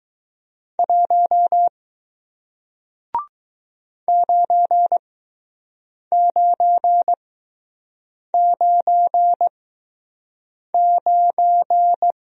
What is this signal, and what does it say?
Morse code, heard as a keyed tone, about 20 wpm, spaced as for 6 wpm, 700 Hz. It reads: 1 9999